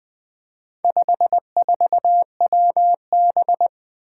Morse code, heard as a keyed tone, about 20 wpm, 700 Hz. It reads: 54WB
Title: Four Challenging Letter-Number Combinations 20wpm